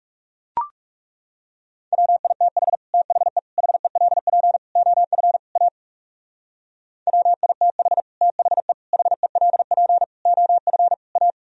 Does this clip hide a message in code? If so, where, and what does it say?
Yes: Morse code from 1.9 to 11.3 s, heard as a keyed tone, about 39 wpm, 700 Hz. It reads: WITH THE HELP OF A WITH THE HELP OF A